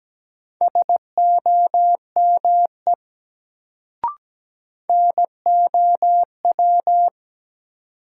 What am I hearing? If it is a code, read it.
Morse code, heard as a keyed tone, about 17 wpm, 700 Hz. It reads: SOME NOW
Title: Sets of 2 Words - New Word Drill - 17wpm